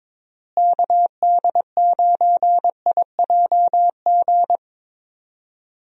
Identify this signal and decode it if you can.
Morse code, heard as a keyed tone, about 22 wpm, 700 Hz. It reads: KD9IJG
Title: USA Call Signs - 6 Characters - Repeated 3x - 22wpm